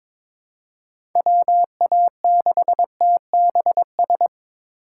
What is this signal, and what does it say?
Morse code, heard as a keyed tone, about 22 wpm, 700 Hz. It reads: WA6TBS
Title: USA Call Signs - 6 Characters - 4x Character Spacing - 22wpm